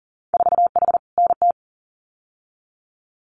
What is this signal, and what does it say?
Morse code, heard as a keyed tone, about 42 wpm, 700 Hz. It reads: 4H NT
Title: Winter Field Day Exchanges Encore 40wpm